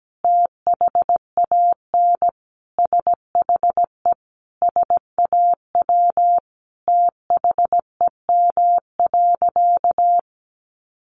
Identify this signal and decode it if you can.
Morse code, heard as a keyed tone, about 17 wpm, 700 Hz. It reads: THAN SHE SAW THEM.
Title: Sentences from Top 1300 Words 17wpm